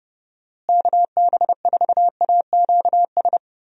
Morse code, heard as a keyed tone, about 30 wpm, 700 Hz. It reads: KB4AQS
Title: USA Call Signs - 6 Characters - Repeated 2x - 30wpm